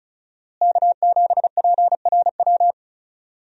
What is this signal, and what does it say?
Morse code, heard as a keyed tone, about 34 wpm, 700 Hz. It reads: K7PRW
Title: USA Call Signs - 5 Characters - Repeated 2x - 35wpm